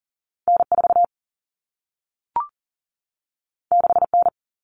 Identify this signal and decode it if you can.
Morse code, heard as a keyed tone, about 42 wpm, 700 Hz. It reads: N4 6N
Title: All Cut Numbers - 2-Digit 40wpm